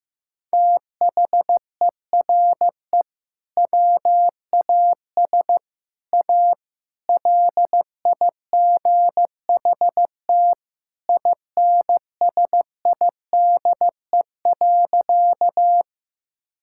Morse code, heard as a keyed tone, about 15 wpm, 700 Hz. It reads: THERE WAS A LIGHT INSIDE.
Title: Sentences from Top 1000 Words - Review Part 6 - 15wpm